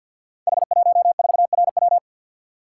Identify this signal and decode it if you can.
Morse code, heard as a keyed tone, about 52 wpm, 700 Hz. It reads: H14RW
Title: Code Groups - 5 Letter:Numbers - 50wpm